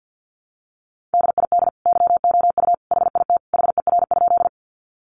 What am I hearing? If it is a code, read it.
Morse code, heard as a keyed tone, about 50 wpm, 700 Hz. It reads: DID YOU HIT HER?
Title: Sentences from Top 1200 Words 50wpm